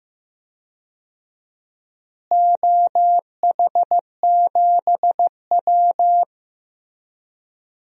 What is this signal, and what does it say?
Morse code, heard as a keyed tone, about 15 wpm, 700 Hz. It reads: OH7W
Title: DXCC-100 Call Signs - 4 Characters - Repeated 2x - 15wpm